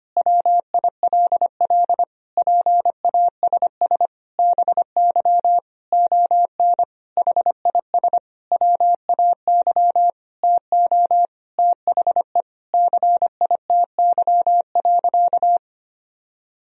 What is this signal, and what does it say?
Morse code, heard as a keyed tone, about 25 wpm, 700 Hz. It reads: WILL PASS BY ON HIS WAY TO THE CITY.
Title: Sentences from Top 1000 Words - Review Part 1 - 25wpm